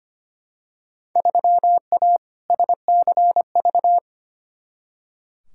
Morse code, heard as a keyed tone, about 25 wpm, 700 Hz. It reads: 3A SCV
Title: Field Day Exchanges - 25wpm